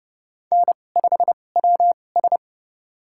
Morse code, heard as a keyed tone, about 30 wpm, spaced as for 22 wpm, 700 Hz. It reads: N5WS
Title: USA Call Signs - 4 Characters - 2x Character Spacing - 30wpm